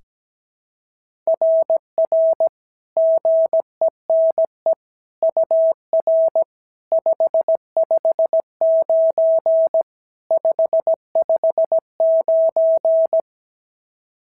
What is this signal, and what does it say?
Morse code, heard as a keyed tone, about 17 wpm, 650 Hz. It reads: RR GENE UR 559 559
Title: SOTA QSO Practice 17wpm